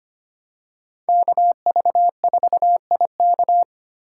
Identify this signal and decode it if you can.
Morse code, heard as a keyed tone, about 25 wpm, 700 Hz. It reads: KV4IK
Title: USA Call Signs - 5 Characters - 4x Character Spacing - 25wpm